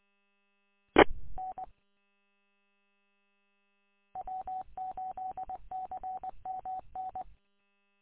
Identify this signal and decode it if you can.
Morse code, heard as a keyed tone, about 20 wpm, 750 Hz. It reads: E N W8CMN
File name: w8cmn- -combined